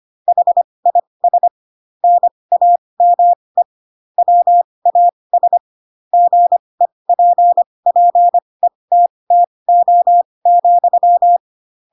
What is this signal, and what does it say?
Morse code, heard as a keyed tone, about 25 wpm, spaced as for 20 wpm, 700 Hz. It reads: HIS NAME WAS GEPPETTO,
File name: Pinocchio / Pinocchio ch2 15 25 ik1hge